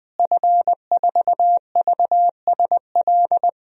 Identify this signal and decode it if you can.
Morse code, heard as a keyed tone, about 20 wpm, 700 Hz. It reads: F4VSL
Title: DXCC-100 Call Signs - 5 Characters - Repeated 2x - 20wpm